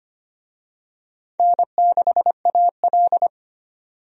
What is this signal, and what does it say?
Morse code, heard as a keyed tone, about 25 wpm, 700 Hz. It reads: N6AL
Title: USA Call Signs - 4 Characters - 25wpm